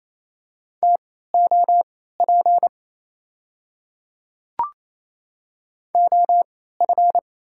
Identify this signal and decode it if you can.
Morse code, heard as a keyed tone, about 28 wpm, spaced as for 16 wpm, 700 Hz. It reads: TOP OF
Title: 3x Character Spacing - Top 500 Words - 28wpm